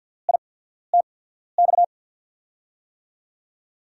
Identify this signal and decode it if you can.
Morse code, heard as a keyed tone, about 47 wpm, spaced as for 13 wpm, 700 Hz. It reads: ITX
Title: USA Call Signs - 6 Characters - 8x Character Spacing - 50wpm